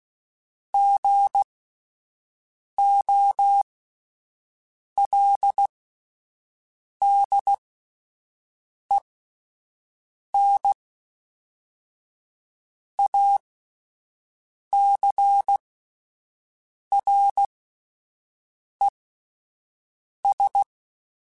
Morse code, heard as a keyed tone, about 16 wpm, spaced as for 5 wpm, 800 Hz. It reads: GOLDENACRES
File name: fromSMs / FromSMs mp3s / Cities@5